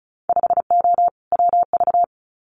Morse code, heard as a keyed tone, about 36 wpm, 700 Hz. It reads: 5O WV
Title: Winter Field Day Exchanges Encore 35wpm